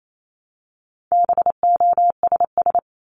Morse code, heard as a keyed tone, about 28 wpm, 700 Hz. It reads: BOSS
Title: Top 501-750 Words 28wpm